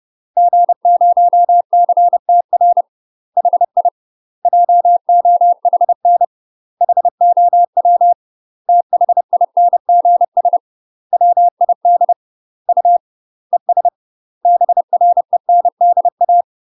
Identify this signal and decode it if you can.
Morse code, heard as a keyed tone, about 30 wpm, 700 Hz. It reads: G0CTR HI JOHN HOW THINGS WID U ES BRENDA
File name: UK MASTER QSOs TRAINING / 30WPM / UK MASTER QSOs 0021